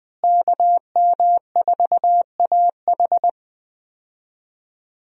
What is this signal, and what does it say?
Morse code, heard as a keyed tone, about 20 wpm, 700 Hz. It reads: KM4AH
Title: USA Call Signs - 5 Characters - Repeated 2x - 20wpm